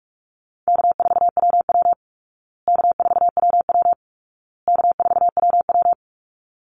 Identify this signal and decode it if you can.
Morse code, heard as a keyed tone, about 44 wpm, 700 Hz. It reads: K4WW K4WW K4WW
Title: Easy Call Signs Repeated 3x - 45wpm